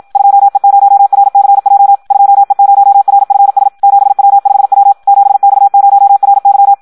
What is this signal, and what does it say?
Morse code, heard as a keyed tone, about 56 wpm, 800 Hz. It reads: 2E0RCJ 2E0RFS 7W5W 7X0RY
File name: Audio practica CW / qrq-tx-54 2